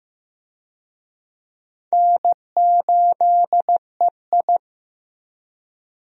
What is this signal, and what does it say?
Morse code, heard as a keyed tone, about 15 wpm, 700 Hz. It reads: N8EI
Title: USA Call Signs - 4 Characters - Repeated 2x - 15wpm